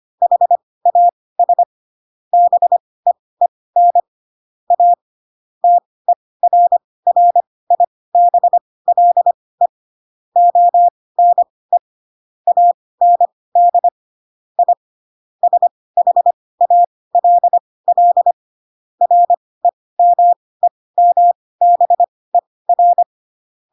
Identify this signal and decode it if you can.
Morse code, heard as a keyed tone, about 25 wpm, spaced as for 18 wpm, 700 Hz. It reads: HAS BEEN A TERRIBLE ONE AND I SHALL REMEMBER
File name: Pinocchio / Pinocchio ch7 12 25 ik1hge